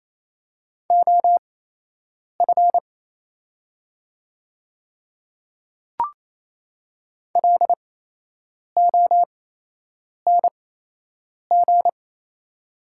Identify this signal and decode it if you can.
Morse code, heard as a keyed tone, about 27 wpm, spaced as for 8 wpm, 690 Hz. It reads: OF LONG
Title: 8x Character Spacing - Top 200 Words - 28wpm